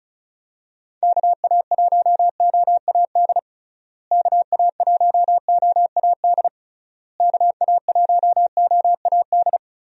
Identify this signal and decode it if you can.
Morse code, heard as a keyed tone, about 34 wpm, 700 Hz. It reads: KA1OAD KA1OAD KA1OAD
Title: USA Call Signs - 6 Characters - Repeated 3x - 35wpm